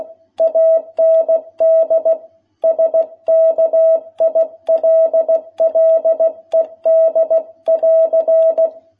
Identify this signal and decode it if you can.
Morse code, heard as a keyed tone, about 16 wpm, 660 Hz. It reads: AND SKILLED <AR>
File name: jh8cbh / 2021cw